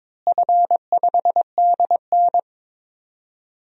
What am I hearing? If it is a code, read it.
Morse code, heard as a keyed tone, about 22 wpm, 700 Hz. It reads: F5DN
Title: DXCC-100 Call Signs - 4 Characters - Repeated 2x - 22wpm